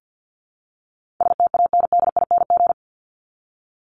Morse code, heard as a keyed tone, about 49 wpm, 700 Hz. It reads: STANDING
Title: ING Suffix - Top 100 Words with Base Words 50wpm